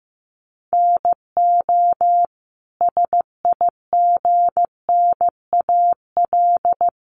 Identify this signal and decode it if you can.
Morse code, heard as a keyed tone, about 15 wpm, 700 Hz. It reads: NO SIGNAL
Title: Sentences from Top 1400 Words 15wpm